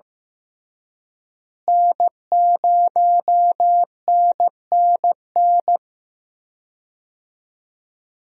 Morse code, heard as a keyed tone, about 15 wpm, 700 Hz. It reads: N0NNN
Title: USA Call Signs - 5 Characters - Repeated 3x - 15wpm